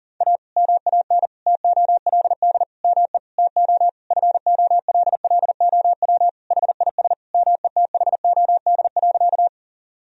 Morse code, heard as a keyed tone, about 37 wpm, 700 Hz. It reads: A MAN TOLD ME TO FOLLOW HIS METHOD.
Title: Sentences from Top 1000 Words - Review Part 3 - 40wpm